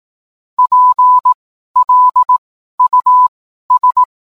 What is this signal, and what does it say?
Morse code, heard as a keyed tone, about 18 wpm, spaced as for 12 wpm, 1000 Hz. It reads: PLUS
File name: cwblog / cw mp3 directory / mem morse 12